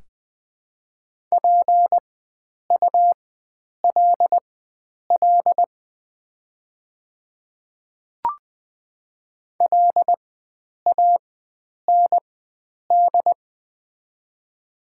Morse code, heard as a keyed tone, about 20 wpm, spaced as for 9 wpm, 700 Hz. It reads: PULL LAND
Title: 4x Character Spacing - Top 500 Words - 20wpm